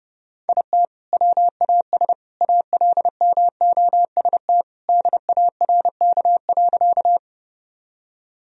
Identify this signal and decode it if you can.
Morse code, heard as a keyed tone, about 32 wpm, 700 Hz. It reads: IT WAS ALMOST DARK.
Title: Sentences from Top 1000 Words - Review Part 4 - 30wpm